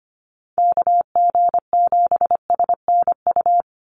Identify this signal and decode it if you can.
Morse code, heard as a keyed tone, about 25 wpm, 700 Hz. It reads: KG7SNU
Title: USA Call Signs 25wpm